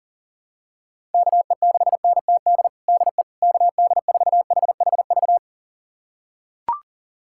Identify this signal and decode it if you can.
Morse code, heard as a keyed tone, about 40 wpm, 700 Hz. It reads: KE6NTD DE KD4HHV
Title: USA Callsign DE Callsign 40wpm